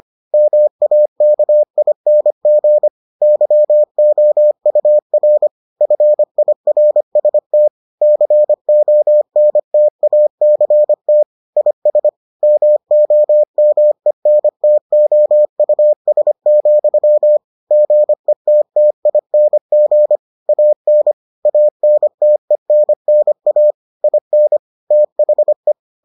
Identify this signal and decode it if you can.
Morse code, heard as a keyed tone, about 25 wpm, 600 Hz. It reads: MAKING YOUR FIRST CONTACT IS MOMENTOUS, GETTING AN ANTENNA IN THE